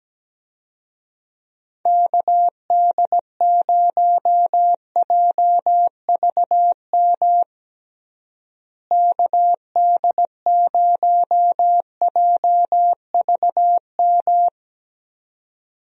Morse code, heard as a keyed tone, about 17 wpm, 700 Hz. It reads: KD0JVM KD0JVM
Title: USA Call Signs - 6 Characters - Repeated 2x - 17wpm